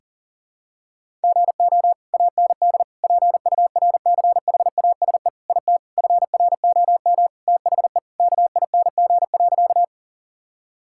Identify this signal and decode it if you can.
Morse code, heard as a keyed tone, about 40 wpm, 700 Hz. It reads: GO AND PURCHASE IT FROM THE KING.
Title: Sentences from Top 1000 Words - Review Part 5 - 40wpm